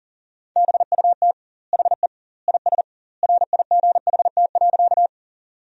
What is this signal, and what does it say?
Morse code, heard as a keyed tone, about 40 wpm, 700 Hz. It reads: BUT HE IS RIGHT.
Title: 2x Word-Spacing - Sentences from Top 200 Words - 40wpm